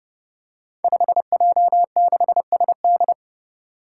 Morse code, heard as a keyed tone, about 30 wpm, 700 Hz. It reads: 5J6SD
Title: Five Challenging Letter-Number Combinations 30wpm